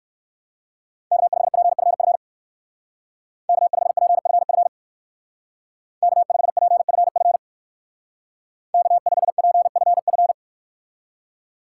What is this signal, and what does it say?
Morse code, heard as a keyed tone, about 59 wpm, 700 Hz. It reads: K5PFF K5PFF K5PFF K5PFF